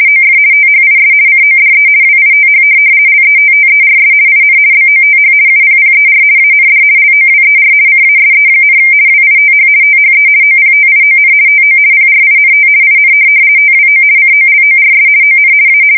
RTTY, about 50 baud. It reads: OF THIS CODE WITH LETTERS IN BOTH UPPER AND LOWER CASE, WITH THE OPTION TO PRINT IN TWO COLORS ON A
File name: fsk170-50B / teletype-corp